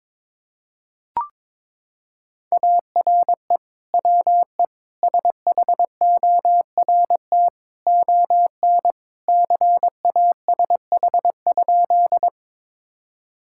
Morse code, heard as a keyed tone, about 22 wpm, 700 Hz. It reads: ARE WE SHORT ON CASH?
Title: Sentences from Top 1000 Words - Review Part 5 - 22wpm